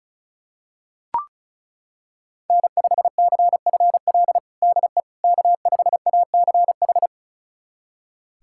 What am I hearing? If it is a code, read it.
Morse code, heard as a keyed tone, about 36 wpm, 700 Hz. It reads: N5CFL DE K5ACH
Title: USA Callsign DE Callsign - 5 Characters - 35wpm